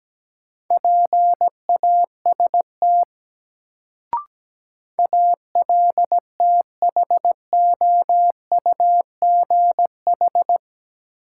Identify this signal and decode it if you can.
Morse code, heard as a keyed tone, about 17 wpm, 700 Hz. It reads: PAST ALTHOUGH
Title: Sentences from Top 600 Words - New Word Drill - 17wpm